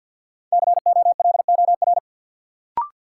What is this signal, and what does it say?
Morse code, heard as a keyed tone, about 52 wpm, 700 Hz. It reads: COLOR
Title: Sentences from Top 1000 Words - Review Part 2 - New Word Drill - 50wpm